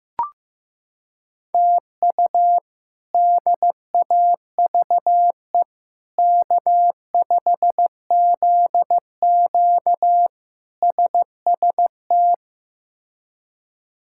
Morse code, heard as a keyed tone, about 16 wpm, 700 Hz. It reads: TU DAVE K5ZQ SST
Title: K1USN SST QSO Practice - 15wpm